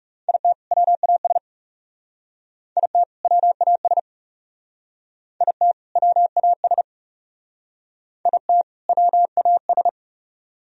Morse code, heard as a keyed tone, about 38 wpm, 700 Hz. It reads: IT WAS IT WAS IT WAS IT WAS
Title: Speed-Racing - Sets of 2 Words 30wpm